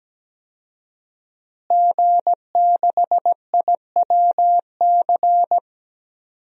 Morse code, heard as a keyed tone, about 17 wpm, 700 Hz. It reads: G6IWC
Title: DXCC-100 Call Signs - 5 Characters - Repeated 3x - 17wpm